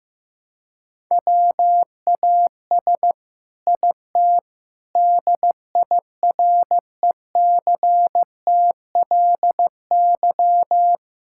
Morse code, heard as a keyed tone, about 15 wpm, 700 Hz. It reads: WAS IT DIRECTLY
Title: Sentences from Top 1000 Words - Review Part 5 - 15wpm